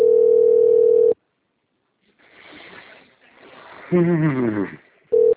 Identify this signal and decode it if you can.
Ringback tone (North America), 440 + 480 Hz, with speech.
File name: out- - - - - .14367